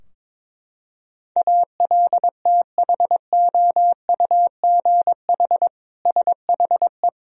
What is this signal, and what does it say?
Morse code, heard as a keyed tone, about 21 wpm, 700 Hz. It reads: ALTHOUGH SHE
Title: Sentences from Top 1000 Words - Review Part 3 - 22wpm